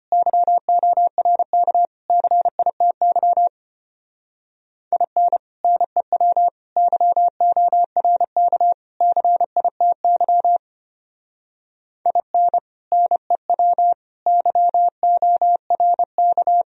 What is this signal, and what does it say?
Morse code, heard as a keyed tone, about 32 wpm, 700 Hz. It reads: YORK CITY IN NEW YORK CITY IN NEW YORK